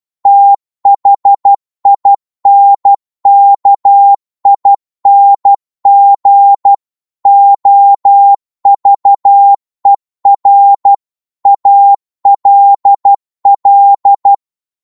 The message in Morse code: THINKING OVER ALL